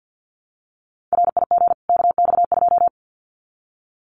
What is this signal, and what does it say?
Morse code, heard as a keyed tone, about 50 wpm, 700 Hz. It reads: RIG KX2
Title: Common QSO Phrases 50wpm